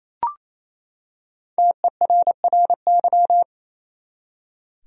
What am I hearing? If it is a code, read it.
Morse code, heard as a keyed tone, about 27 wpm, 700 Hz. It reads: TERRY
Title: USA Common Names - Max 5 Letters - 28wpm